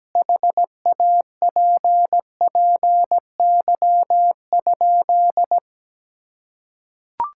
Morse code, heard as a keyed tone, about 17 wpm, 700 Hz. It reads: HAPPY?
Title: Sentences from Top 1000 Words - Review Part 6 - 17wpm